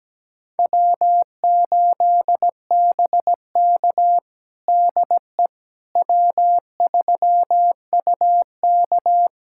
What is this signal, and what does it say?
Morse code, heard as a keyed tone, about 17 wpm, 700 Hz. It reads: W8BK DE W3UK
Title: USA Callsign DE Callsign - 4 Characters - 17wpm